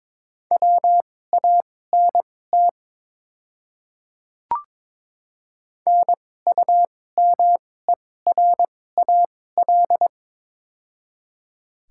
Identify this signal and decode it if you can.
Morse code, heard as a keyed tone, about 22 wpm, spaced as for 16 wpm, 690 Hz. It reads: WANT NUMERAL
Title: 2x Character Spacing - Top 500 Words - 22wpm